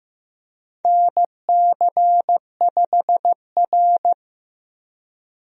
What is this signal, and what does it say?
Morse code, heard as a keyed tone, about 15 wpm, 700 Hz. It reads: NC5R
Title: USA Call Signs - 4 Characters - 15wpm